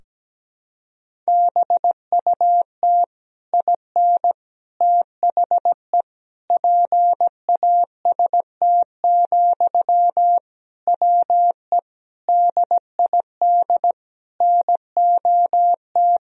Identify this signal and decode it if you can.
Morse code, heard as a keyed tone, about 17 wpm, 700 Hz. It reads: BUT IN THE PAST, WE DID NOT
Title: Sentences from Top 1000 Words - Review Part 3 - 17wpm